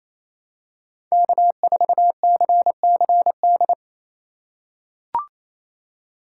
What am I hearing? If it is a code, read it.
Morse code, heard as a keyed tone, about 28 wpm, 700 Hz. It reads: K4CCD
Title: USA Call Signs - 5 Characters - 2x Character Spacing - 28wpm